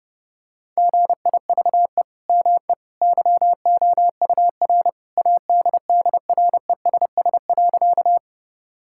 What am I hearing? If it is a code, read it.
Morse code, heard as a keyed tone, about 30 wpm, 700 Hz. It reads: GIVE ME YOUR ADDRESS.
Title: Sentences from Top 1000 Words - Review Part 6 - 30wpm